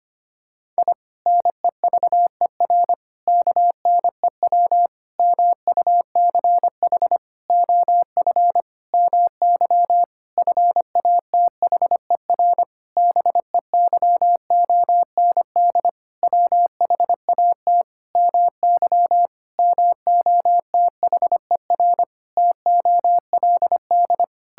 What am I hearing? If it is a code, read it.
Morse code, heard as a keyed tone, about 25 wpm, 700 Hz. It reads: I NEVER KNEW MUCH OF MY FATHER BEYOND WHAT MY MOTHER TOLD
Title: Sentences from Top 1000 Words - Review Part 6 - 25wpm